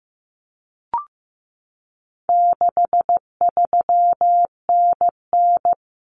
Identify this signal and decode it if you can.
Morse code, heard as a keyed tone, about 15 wpm, 700 Hz. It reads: 63NN
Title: Common Cut Numbers - 4-Digit 15wpm